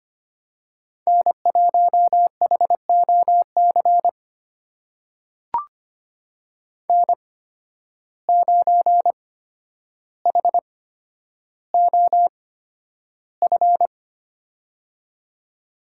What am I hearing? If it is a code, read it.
Morse code, heard as a keyed tone, about 25 wpm, spaced as for 11 wpm, 700 Hz. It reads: N1HOC N9HOF
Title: USA Call Signs - 5 Characters - 8x Character Spacing - 25wpm